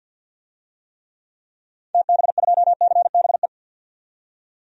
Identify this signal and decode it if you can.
Morse code, heard as a keyed tone, about 47 wpm, 700 Hz. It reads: TB2CBE
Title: DXCC-100 Call Signs - 6 Characters - 50wpm